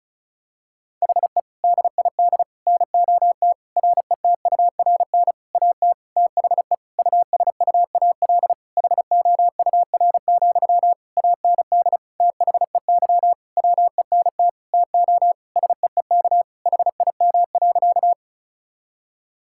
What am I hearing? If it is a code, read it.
Morse code, heard as a keyed tone, about 36 wpm, 700 Hz. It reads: HE DID NOT RETURN AT THE USUAL HOUR, AND THEY WENT TO SEEK HIM.
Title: Sentences from Top 1600 Words 35wpm